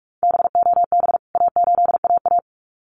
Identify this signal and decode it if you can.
Morse code, heard as a keyed tone, about 44 wpm, 700 Hz. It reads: BOB A8AA
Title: CWT Exchange 45wpm